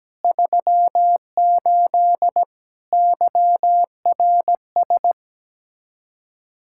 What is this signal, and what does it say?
Morse code, heard as a keyed tone, about 17 wpm, 700 Hz. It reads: 38 YRS
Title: Practice QSO - Exchange 3 - More Info - 17wpm